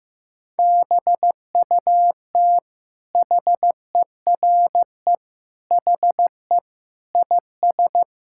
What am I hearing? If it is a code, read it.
Morse code, heard as a keyed tone, about 15 wpm, 700 Hz. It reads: BUT HERE HE IS
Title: Sentences from Top 1600 Words 15wpm